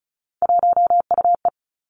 Morse code, heard as a keyed tone, about 36 wpm, 700 Hz. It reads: 1UE
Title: Three Challenging Letter-Number Combinations 35wpm